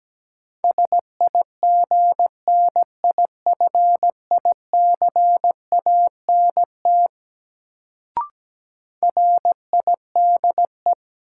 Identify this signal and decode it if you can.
Morse code, heard as a keyed tone, about 17 wpm, 700 Hz. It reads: SIGNIFICANT RIDE
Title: Sentences from Top 1000 Words - Review Part 6 - New Word Drill - 17wpm